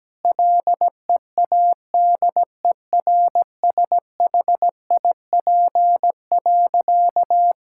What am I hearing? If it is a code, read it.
Morse code, heard as a keyed tone, about 17 wpm, 700 Hz. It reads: LEADERSHIP.
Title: Sentences from Top 1600 Words 17wpm